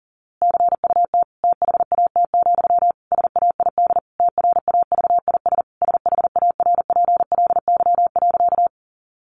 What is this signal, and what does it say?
Morse code, heard as a keyed tone, about 40 wpm, 700 Hz. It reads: CUT THAT, SAID TRAVIS SHARPLY.
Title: A Sound Of Thunder 40wpm